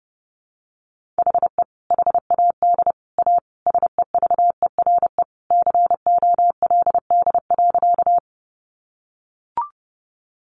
Morse code, heard as a keyed tone, about 28 wpm, 700 Hz. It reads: HE HAD A SEVERE COLD.
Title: Sentences from Top 1400 Words 30wpm